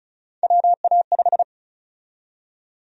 Morse code, heard as a keyed tone, about 36 wpm, 700 Hz. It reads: WA5
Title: USA Call Sign Prefixes - 35wpm